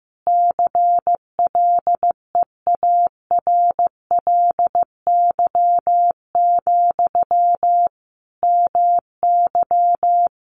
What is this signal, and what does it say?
Morse code, heard as a keyed tone, about 15 wpm, 700 Hz. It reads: CLEARLY, MY
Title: Sentences from Top 1300 Words 15wpm